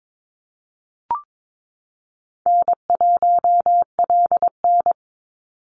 Morse code, heard as a keyed tone, about 22 wpm, 700 Hz. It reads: N1LN
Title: Easy Call Signs Repeated 2x - 22wpm